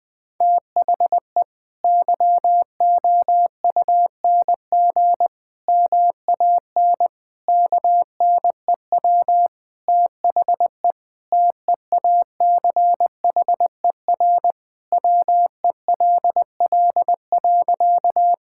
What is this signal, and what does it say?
Morse code, heard as a keyed tone, about 20 wpm, 700 Hz. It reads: THE YOUNG MAN KNEW THE TEACHER WELL.
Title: Sentences from Top 1000 Words - Review Part 6 - 20wpm